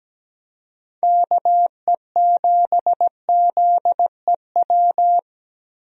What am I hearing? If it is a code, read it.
Morse code, heard as a keyed tone, about 17 wpm, 700 Hz. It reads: KE7ZEW
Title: USA Call Signs - 6 Characters - Repeated 3x - 17wpm